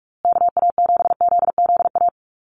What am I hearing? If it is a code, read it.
Morse code, heard as a keyed tone, about 44 wpm, 700 Hz. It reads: KA7ZZA
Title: USA Call Signs Repeated 3x 45wpm